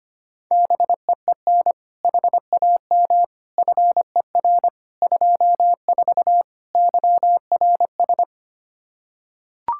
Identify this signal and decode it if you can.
Morse code, heard as a keyed tone, about 25 wpm, 700 Hz. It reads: BEEN HAM FER 24 YRS
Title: Practice QSO - Exchange 3 - More Info - 25wpm